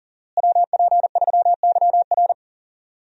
Morse code, heard as a keyed tone, about 40 wpm, 700 Hz. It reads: WP3YR
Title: USA Call Signs - 5 Characters - Repeated 2x - 40wpm